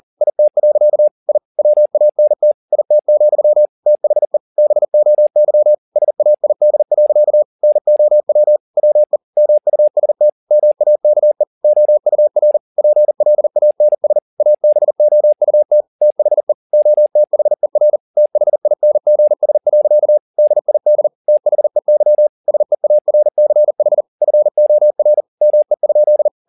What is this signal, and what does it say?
Morse code, heard as a keyed tone, about 40 wpm, 600 Hz. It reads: IT. I WANT IT, THE BOY SAID. NOW WE MUST MAKE OUR PLANS ABOUT THE OTHER THINGS. DID THEY SEARCH FOR ME?
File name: the old man and the sea 2 40wpm